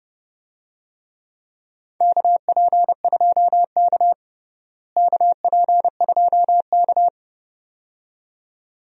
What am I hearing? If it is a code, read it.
Morse code, heard as a keyed tone, about 30 wpm, 700 Hz. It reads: KP2K KP2K
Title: DXCC-100 Call Signs - 4 Characters - Repeated 2x - 30wpm